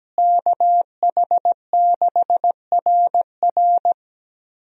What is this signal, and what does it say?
Morse code, heard as a keyed tone, about 17 wpm, 700 Hz. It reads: KH6RR
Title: USA Call Signs - 5 Characters - Repeated 2x - 17wpm